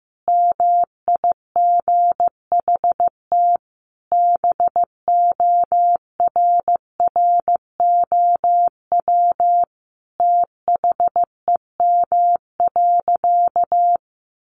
Morse code, heard as a keyed tone, about 15 wpm, 700 Hz. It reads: MIGHT BORROW THEM.
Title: Sentences from Top 1500 Words 15wpm